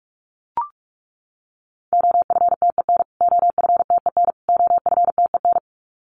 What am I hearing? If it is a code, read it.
Morse code, heard as a keyed tone, about 44 wpm, 700 Hz. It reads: OFTEN OFTEN OFTEN
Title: Top 500 Words Repeated 45wpm